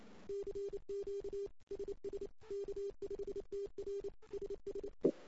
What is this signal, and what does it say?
Morse code, heard as a keyed tone, about 28 wpm, 410 Hz. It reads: CQ SS K5TR SSE
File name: CW / 2008-ss-cw-k5tr-op-n5rz -05:00